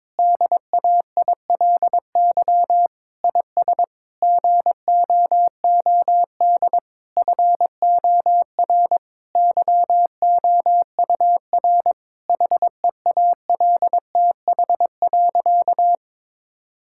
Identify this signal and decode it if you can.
Morse code, heard as a keyed tone, about 22 wpm, 700 Hz. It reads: DAILY IS GOOD FOR YOUR HEALTH.
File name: Difficult Sentences with Hints - 22wpm